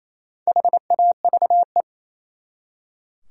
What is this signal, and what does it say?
Morse code, heard as a keyed tone, about 34 wpm, 700 Hz. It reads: HAVE
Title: Speed-Racing Top 200 Words 28wpm